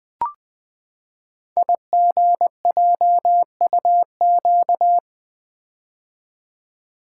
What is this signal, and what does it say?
Morse code, heard as a keyed tone, about 20 wpm, 700 Hz. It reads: IGJUQ
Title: Code Groups - 5 Letters - 20wpm